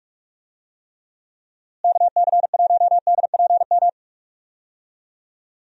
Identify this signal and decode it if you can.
Morse code, heard as a keyed tone, about 44 wpm, 700 Hz. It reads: KC1DPM
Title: USA Call Signs - 6 Characters - Repeated 3x - 45wpm